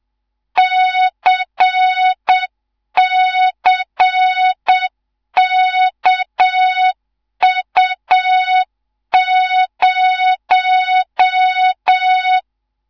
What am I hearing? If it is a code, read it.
Morse code, heard as a keyed tone, about 7 wpm, 3700 Hz, with noise. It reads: CCKU0
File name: Morze-HA2MN / hang / 19 GYAKORLAT